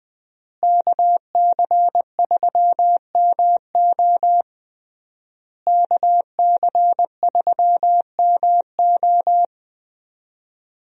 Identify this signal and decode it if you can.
Morse code, heard as a keyed tone, about 21 wpm, 700 Hz. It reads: KC3MO KC3MO
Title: USA Call Signs - 5 Characters - Repeated 3x - 20wpm